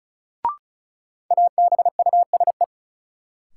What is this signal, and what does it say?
Morse code, heard as a keyed tone, about 36 wpm, 700 Hz. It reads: ABUSE
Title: Sentences from Top 1300 Words - New Word Drill - 35wpm